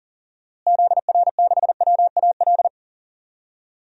Morse code, heard as a keyed tone, about 37 wpm, 700 Hz. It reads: ZR6WAL
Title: DXCC-100 Call Signs - 6 Characters - Repeated 3x - 40wpm